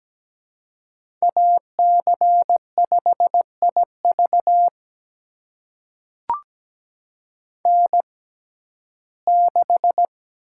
Morse code, heard as a keyed tone, about 17 wpm, spaced as for 12 wpm, 700 Hz. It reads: AC5IV N6
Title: USA Call Signs - 6x Character Spacing - 17wpm